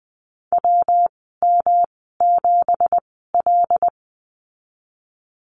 Morse code, heard as a keyed tone, about 20 wpm, spaced as for 15 wpm, 700 Hz. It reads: WM7L
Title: 219 Lesson 35g - Number 0 - Call Signs 20wpm